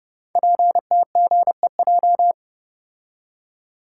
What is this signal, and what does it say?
Morse code, heard as a keyed tone, about 28 wpm, 700 Hz. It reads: PTGEJ
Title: Code Groups - 5 Letters - 30wpm